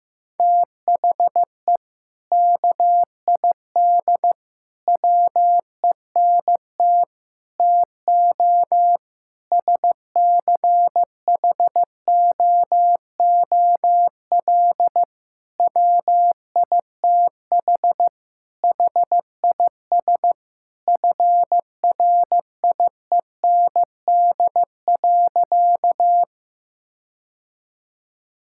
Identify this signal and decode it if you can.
Morse code, heard as a keyed tone, about 15 wpm, 700 Hz. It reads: THE KID WENT TO SCHOOL WITH HIS FRIEND.
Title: Sentences from Top 1600 Words 15wpm